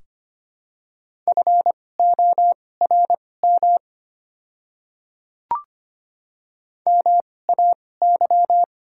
Morse code, heard as a keyed tone, about 25 wpm, spaced as for 18 wpm, 690 Hz. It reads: FORM MAY
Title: 2x Character Spacing - Top 300 Words - 25wpm